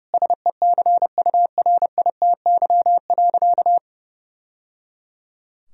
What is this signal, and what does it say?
Morse code, heard as a keyed tone, about 30 wpm, 700 Hz. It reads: SECURITY.
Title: Sentences from Top 1000 Words - Review Part 4 - 30wpm